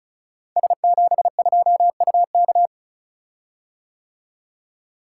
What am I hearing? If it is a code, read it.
Morse code, heard as a keyed tone, about 36 wpm, 690 Hz. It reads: S72UK
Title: Code Groups - 5 Letter:Numbers - 35wpm